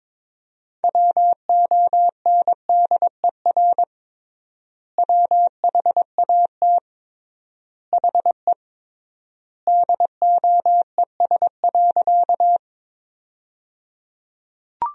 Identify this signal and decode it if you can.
Morse code, heard as a keyed tone, about 22 wpm, 700 Hz. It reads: WONDER WHAT HE DOES.
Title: 3x Word-Spacing - Sentences from Top 500 Words - 22wpm